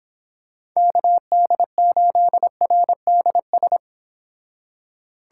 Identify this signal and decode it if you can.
Morse code, heard as a keyed tone, about 26 wpm, 700 Hz. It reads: KD8RDS